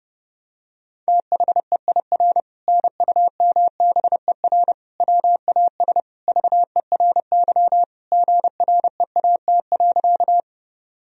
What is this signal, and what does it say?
Morse code, heard as a keyed tone, about 32 wpm, 700 Hz. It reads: THEIR NUMBER WAS VERY GREAT.
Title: Sentences from Top 1000 Words - Review Part 1 - 30wpm